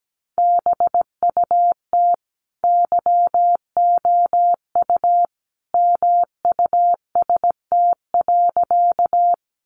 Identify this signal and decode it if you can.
Morse code, heard as a keyed tone, about 17 wpm, 700 Hz. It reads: BUT YOU MUST.
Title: Sentences from Top 200 Words 17wpm